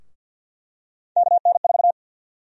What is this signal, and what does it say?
Morse code, heard as a keyed tone, about 52 wpm, 700 Hz. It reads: KN4
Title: USA Call Sign Prefixes - Repeated 2x - 50wpm